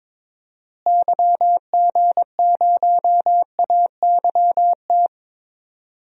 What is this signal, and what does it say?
Morse code, heard as a keyed tone, about 21 wpm, 700 Hz. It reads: YG0AYT
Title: DXCC-100 Call Signs - 6 Characters - Repeated 3x - 22wpm